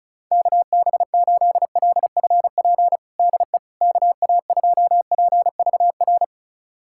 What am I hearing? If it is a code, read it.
Morse code, heard as a keyed tone, about 36 wpm, 700 Hz. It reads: KB8LFP DE KA2PVR
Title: USA Callsign DE Callsign - 6 Characters - 35wpm